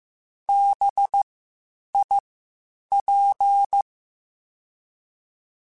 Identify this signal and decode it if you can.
Morse code, heard as a keyed tone, about 15 wpm, spaced as for 8 wpm, 800 Hz. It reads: BIP
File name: Spaced Alphas / Spaced Alphas mp3s / Bs sp @13